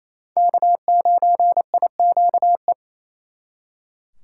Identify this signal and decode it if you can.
Morse code, heard as a keyed tone, about 34 wpm, 700 Hz. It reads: K9IQE
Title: Speed Racing - USA 1x3 Call Signs 28wpm